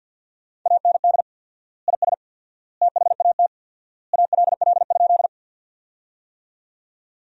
Morse code, heard as a keyed tone, about 52 wpm, 700 Hz. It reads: AND IS THAT ALL?
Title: 4x Word-Spacing - Sentences from Top 100 Words - 50wpm